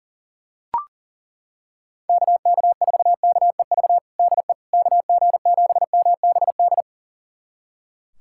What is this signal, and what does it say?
Morse code, heard as a keyed tone, about 40 wpm, 700 Hz. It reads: KK4KEV DE KG7MBD
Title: USA Callsign DE Callsign - 6 Characters - 40wpm